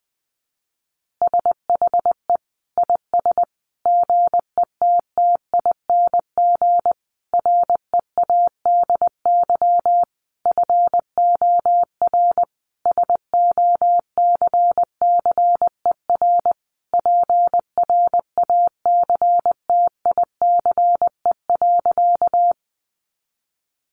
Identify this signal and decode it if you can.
Morse code, heard as a keyed tone, about 20 wpm, 700 Hz. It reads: SHE IS GETTING READY FOR SOCCER PRACTICE.
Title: Easy Phrases 20wpm